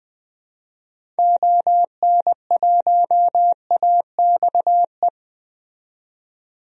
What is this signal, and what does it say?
Morse code, heard as a keyed tone, about 20 wpm, 700 Hz. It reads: ON1AXE
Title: DXCC-100 Call Signs - 6 Characters - 20wpm